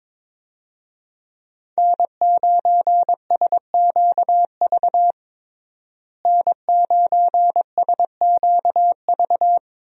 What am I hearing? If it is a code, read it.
Morse code, heard as a keyed tone, about 21 wpm, 700 Hz. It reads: N9SQV N9SQV
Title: USA Call Signs - 5 Characters - Repeated 2x - 22wpm